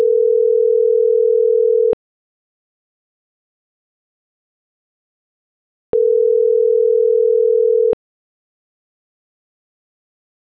Ringback tone (North America), 440 + 480 Hz.